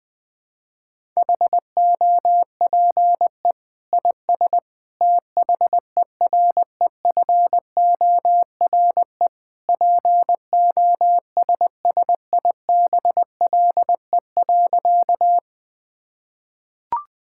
Morse code, heard as a keyed tone, about 20 wpm, 700 Hz. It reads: HOPE IS THEREFORE POSSIBLE.
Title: Sentences from Top 1000 Words - Review Part 5 - 20wpm